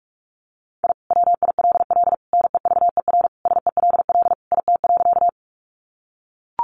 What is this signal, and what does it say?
Morse code, heard as a keyed tone, about 44 wpm, 700 Hz. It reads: I WILL NEVER SELL IT.
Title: Sentences from Top 1100 Words 45wpm